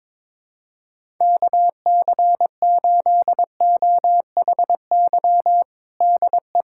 Morse code, heard as a keyed tone, about 22 wpm, 700 Hz. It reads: KC8OHY DE
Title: USA Callsign DE Callsign 22wpm